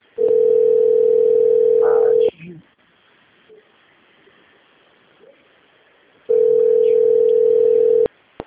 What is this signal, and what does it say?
Ringback tone (North America), 440 + 480 Hz.